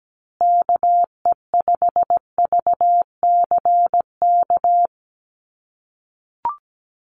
Morse code, heard as a keyed tone, about 17 wpm, 700 Hz. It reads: KE5VCK
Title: USA Call Signs Repeated 2x 17wpm